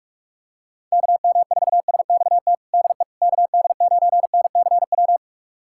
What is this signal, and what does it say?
Morse code, heard as a keyed tone, about 45 wpm, 700 Hz. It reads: KM4SXT DE KD9NCW
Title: USA Callsign DE Callsign - 6 Characters - 45wpm